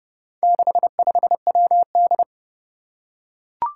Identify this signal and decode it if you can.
Morse code, heard as a keyed tone, about 28 wpm, 700 Hz. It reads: 65WD
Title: Four Challenging Letter-Number Combinations 30wpm